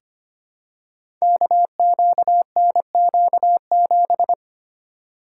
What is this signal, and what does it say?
Morse code, heard as a keyed tone, about 25 wpm, 700 Hz. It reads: KQNQ7
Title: Code Groups - 5 Letter:Numbers - 25wpm